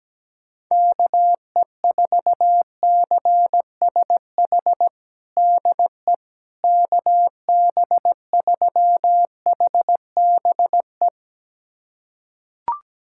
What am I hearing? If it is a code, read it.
Morse code, heard as a keyed tone, about 17 wpm, 700 Hz. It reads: KE4CSH DE KB3HBE
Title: USA Callsign DE Callsign 17wpm